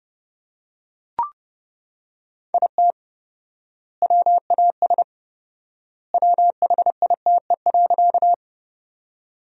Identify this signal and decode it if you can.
Morse code, heard as a keyed tone, about 30 wpm, 700 Hz. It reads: IT WAS WHITE.
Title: 4x Word-Spacing - Sentences from Top 300 Words - 30wpm